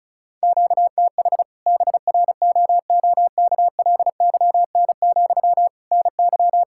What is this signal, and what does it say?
Morse code, heard as a keyed tone, about 36 wpm, 700 Hz. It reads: QTH BROOKLYN, NY
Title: Practice QSO - Exchange 1 - Beginning - 35wpm